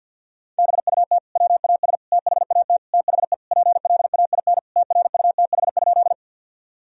Morse code, heard as a keyed tone, about 52 wpm, 700 Hz. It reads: BUT WAS THAT THE PLAIN TRUTH?
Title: Sentences from Top 1000 Words - Review Part 2 - 50wpm